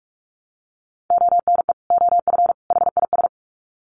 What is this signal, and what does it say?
Morse code, heard as a keyed tone, about 44 wpm, 700 Hz. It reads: ONE OF HIS
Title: Sets of 3 Words 45wpm